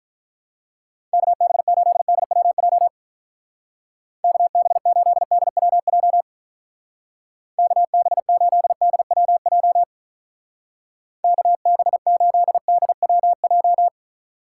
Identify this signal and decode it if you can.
Morse code, heard as a keyed tone, about 44 wpm, 700 Hz. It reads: KB8DWJ KB8DWJ KB8DWJ KB8DWJ